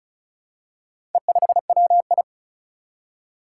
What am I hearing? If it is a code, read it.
Morse code, heard as a keyed tone, about 36 wpm, 700 Hz. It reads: E5WI
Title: Four Challenging Letter-Number Combinations 35wpm